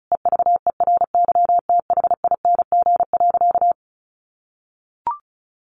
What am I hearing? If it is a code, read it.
Morse code, heard as a keyed tone, about 34 wpm, 700 Hz. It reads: EVERYTHING.
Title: Sentences from Top 900 Words 35wpm